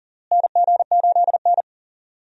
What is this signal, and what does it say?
Morse code, heard as a keyed tone, about 37 wpm, 700 Hz. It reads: NG8N
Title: USA Call Signs - 4 Characters - Repeated 2x - 40wpm